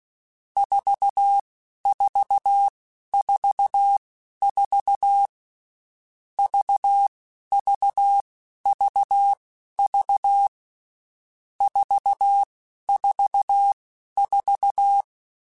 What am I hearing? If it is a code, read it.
Morse code, heard as a keyed tone, about 16 wpm, spaced as for 11 wpm, 800 Hz. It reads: 4444 VVVV 444
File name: Groups@10 / Groups@10 mp3s / VB46@10